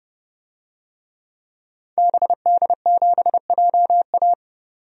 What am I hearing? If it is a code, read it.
Morse code, heard as a keyed tone, about 30 wpm, 700 Hz. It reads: BD7JA
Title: DXCC-100 Call Signs - 5 Characters - 30wpm